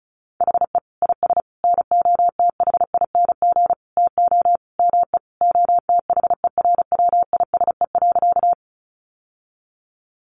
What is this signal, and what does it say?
Morse code, heard as a keyed tone, about 34 wpm, 700 Hz. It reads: HE IS NOTHING TO ME OTHERWISE.
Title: Sentences from Top 1200 Words 35wpm